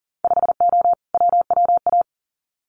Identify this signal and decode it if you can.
Morse code, heard as a keyed tone, about 40 wpm, 700 Hz. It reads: 5O WWA
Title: Winter Field Day Exchanges Encore 40wpm